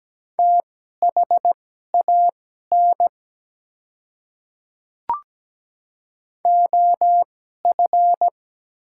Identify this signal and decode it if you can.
Morse code, heard as a keyed tone, about 17 wpm, spaced as for 12 wpm, 700 Hz. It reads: THAN OF
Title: 2x Character Spacing - Top 100 Words - 17wpm